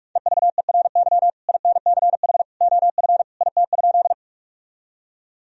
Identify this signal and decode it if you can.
Morse code, heard as a keyed tone, about 41 wpm, 700 Hz. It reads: EVERY INCH OF IT?
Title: Sentences from Top 1000 Words - Review Part 3 - 45wpm